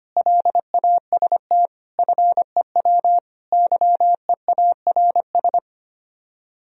Morse code, heard as a keyed tone, about 25 wpm, 700 Hz. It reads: LAST FEW YEARS
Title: Speed-Racing - Sets of 5 Words 25wpm